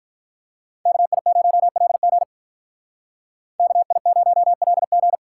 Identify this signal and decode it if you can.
Morse code, heard as a keyed tone, about 45 wpm, 700 Hz. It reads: KI0LG KI0LG